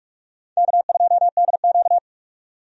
Morse code, heard as a keyed tone, about 46 wpm, 700 Hz. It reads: K2DQ
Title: USA Call Signs - 4 Characters - 45wpm